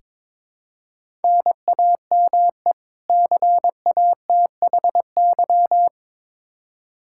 Morse code, heard as a keyed tone, about 22 wpm, 700 Hz. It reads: NAME CATHY
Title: Practice QSO - Exchange 1 - Beginning - 22wpm